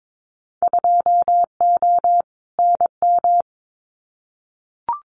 Morse code, heard as a keyed tone, about 22 wpm, 700 Hz. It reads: 2O NM
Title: Winter Field Day Exchanges 22wpm